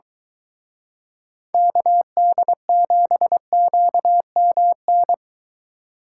Morse code, heard as a keyed tone, about 23 wpm, 700 Hz. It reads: KD7QMN